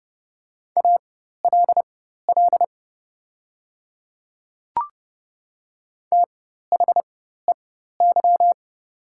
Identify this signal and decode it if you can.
Morse code, heard as a keyed tone, about 30 wpm, spaced as for 14 wpm, 700 Hz. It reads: ALL THEY
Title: 4x Character Spacing - Top 100 Words - 30wpm